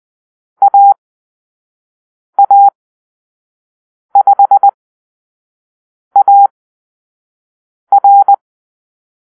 Morse code, heard as a keyed tone, about 20 wpm, spaced as for 5 wpm, 800 Hz. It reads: AA5AR
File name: Call-Sign Drill